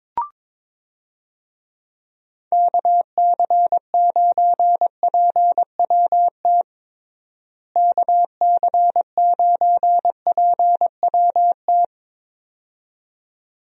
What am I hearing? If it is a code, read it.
Morse code, heard as a keyed tone, about 22 wpm, 700 Hz. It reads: KC9PWT KC9PWT
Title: USA Call Signs - 6 Characters - Repeated 2x - 22wpm